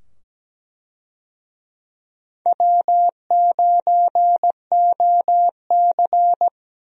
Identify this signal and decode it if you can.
Morse code, heard as a keyed tone, about 17 wpm, 700 Hz. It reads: W9OC
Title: USA Call Signs - 4 Characters - 17wpm